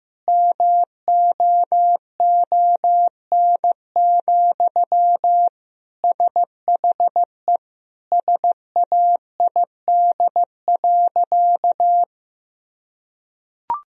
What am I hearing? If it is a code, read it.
Morse code, heard as a keyed tone, about 15 wpm, 700 Hz. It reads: MOON, SHE SAID.
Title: Sentences from Top 1000 Words - Review Part 3 - 15wpm